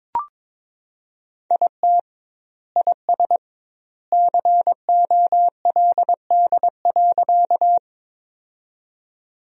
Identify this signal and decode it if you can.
Morse code, heard as a keyed tone, about 22 wpm, 700 Hz. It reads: IT IS COLD.
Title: 2x Word-Spacing - Sentences from Top 400 Words - 22wpm